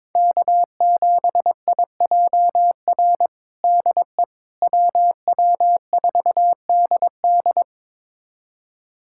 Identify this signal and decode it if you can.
Morse code, heard as a keyed tone, about 22 wpm, 700 Hz. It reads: K7IJR DE WW4DD
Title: USA Callsign DE Callsign - 5 Characters - 22wpm